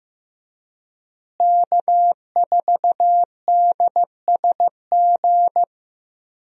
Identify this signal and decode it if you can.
Morse code, heard as a keyed tone, about 15 wpm, 700 Hz. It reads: K4DSG
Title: USA Call Signs - 5 Characters - Repeated 2x - 15wpm